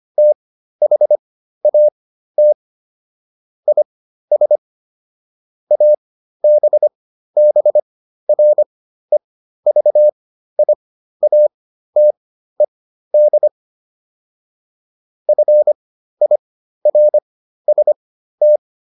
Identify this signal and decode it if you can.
Morse code, heard as a keyed tone, about 25 wpm, spaced as for 13 wpm, 600 Hz. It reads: THAT IS ABBREVIATED FIRST